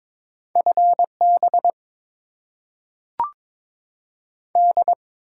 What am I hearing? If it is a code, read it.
Morse code, heard as a keyed tone, about 22 wpm, 700 Hz. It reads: FB D
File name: Top 90 QSO Elements - 6x Character Spacing - 22wpm